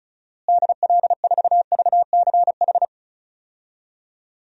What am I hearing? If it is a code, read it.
Morse code, heard as a keyed tone, about 36 wpm, 700 Hz. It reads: DL4VCH
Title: DXCC-100 Call Signs - 6 Characters - Repeated 2x - 35wpm